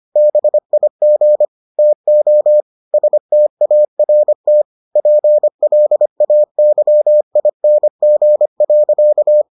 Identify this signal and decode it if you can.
Morse code, heard as a keyed tone, about 25 wpm, 600 Hz. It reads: BIG TO START PLAYING.